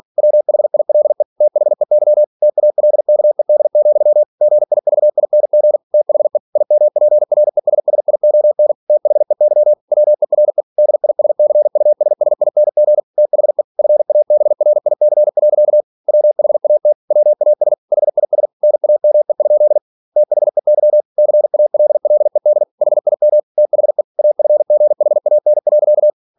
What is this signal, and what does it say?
Morse code, heard as a keyed tone, about 46 wpm, 600 Hz. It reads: WHILE THEY TALKED, GIVING THE IMPRESSION THEY WERE DISCUSSING THE FABRIC. WHAT WAS HIS NAME? THEY CALLED HIM THE AFGHAN.
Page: Never 47wpm